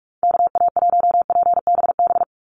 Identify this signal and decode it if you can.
Morse code, heard as a keyed tone, about 44 wpm, 700 Hz. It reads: KA1PBB
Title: USA Call Signs 45wpm